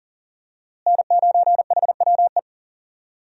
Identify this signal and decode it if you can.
Morse code, heard as a keyed tone, about 40 wpm, 700 Hz. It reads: N9HWE
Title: USA Call Signs - 5 Characters - Repeated 2x - 40wpm